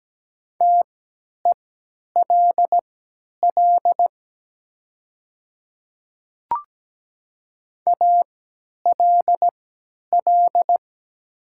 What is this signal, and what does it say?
Morse code, heard as a keyed tone, about 17 wpm, spaced as for 10 wpm, 690 Hz. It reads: TELL ALL
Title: 3x Character Spacing - Top 500 Words - 17wpm